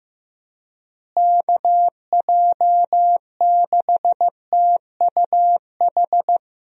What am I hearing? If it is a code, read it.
Morse code, heard as a keyed tone, about 16 wpm, 700 Hz. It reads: KJ6TUH
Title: USA Call Signs - 6 Characters - Repeated 3x - 15wpm